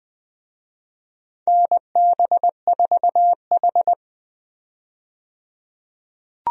Morse code, heard as a keyed tone, about 20 wpm, 700 Hz. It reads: NB4H
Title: USA Call Signs - 4 Characters - 20wpm